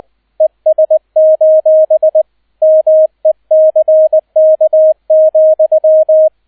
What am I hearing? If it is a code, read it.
Morse code, heard as a keyed tone, about 20 wpm, 630 Hz, with radio noise. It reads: ES: MECK,